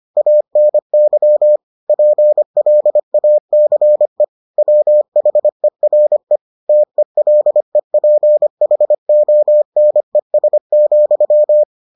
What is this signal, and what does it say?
Morse code, heard as a keyed tone, about 25 wpm, 600 Hz. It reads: ANY PLACE WHERE TELEPHONES,